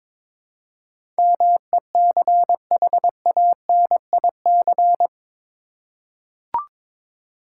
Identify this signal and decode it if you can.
Morse code, heard as a keyed tone, about 21 wpm, 700 Hz. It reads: MECHANIC
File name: practice / 22wpm / Eight Letter Words - 22wpm